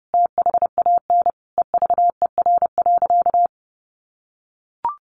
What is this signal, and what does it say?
Morse code, heard as a keyed tone, about 30 wpm, 700 Hz. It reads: THAN EVER.
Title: Sentences from Top 1200 Words 30wpm